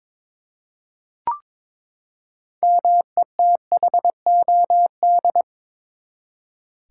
Morse code, heard as a keyed tone, about 21 wpm, 700 Hz. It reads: METHOD
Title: Six Letter Words - 22wpm